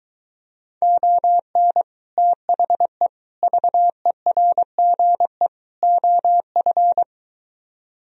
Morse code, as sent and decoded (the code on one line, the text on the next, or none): --- -. / - .... / ...- . .-. --. / --- ..-.
ON THE VERGE OF